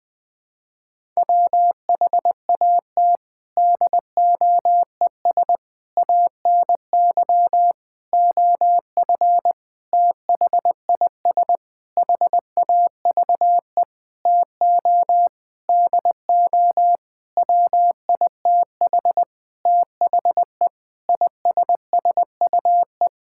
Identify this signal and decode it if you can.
Morse code, heard as a keyed tone, about 20 wpm, 700 Hz. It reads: WHAT DOES ANY OF THIS HAVE TO DO WITH THE ISSUE
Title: Sentences with 9-Word Phrases - 20wpm